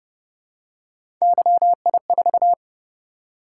Morse code, heard as a keyed tone, about 30 wpm, 700 Hz. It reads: YI4